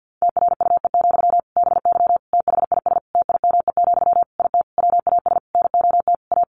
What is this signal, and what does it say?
Morse code, heard as a keyed tone, about 50 wpm, 700 Hz. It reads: TRUE, BY THIS TIME, IT WAS NOT A
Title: Sentences from Top 1500 Words 50wpm